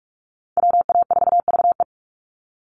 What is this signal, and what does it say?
Morse code, heard as a keyed tone, about 44 wpm, 690 Hz. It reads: WA4VE
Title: USA Call Signs 45wpm